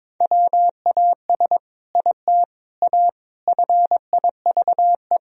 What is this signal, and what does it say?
Morse code, heard as a keyed tone, about 22 wpm, 700 Hz. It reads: WAS IT A FIVE
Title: Sentences from Top 1000 Words - Review Part 2 - 22wpm